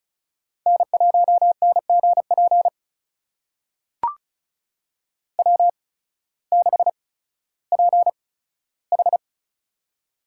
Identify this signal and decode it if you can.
Morse code, heard as a keyed tone, about 34 wpm, spaced as for 16 wpm, 700 Hz. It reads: N1NGP W6PH
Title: USA Call Signs - 8x Character Spacing - 35wpm